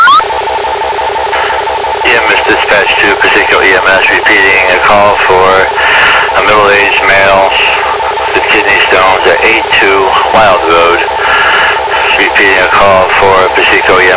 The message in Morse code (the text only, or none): none